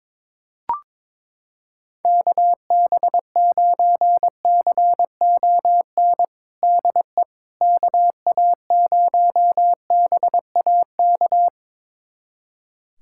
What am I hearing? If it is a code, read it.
Morse code, heard as a keyed tone, about 22 wpm, 700 Hz. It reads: KB9CON DE KA0BAK
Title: USA Callsign DE Callsign 22wpm